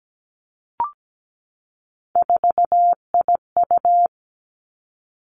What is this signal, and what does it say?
Morse code, heard as a keyed tone, about 17 wpm, 700 Hz. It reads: 4IU
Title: Three Challenging Letter-Number Combinations 17wpm